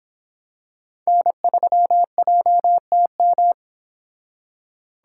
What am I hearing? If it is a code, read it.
Morse code, heard as a keyed tone, about 25 wpm, 700 Hz. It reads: N3JTM